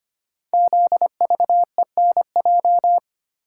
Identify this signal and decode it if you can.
Morse code, heard as a keyed tone, about 25 wpm, 700 Hz. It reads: ZVENJ
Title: Code Groups - 5 Letters - 25wpm